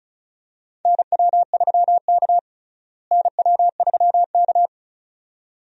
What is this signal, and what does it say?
Morse code, heard as a keyed tone, about 36 wpm, 700 Hz. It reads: NW3K NW3K
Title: USA Call Signs - 4 Characters - Repeated 2x - 35wpm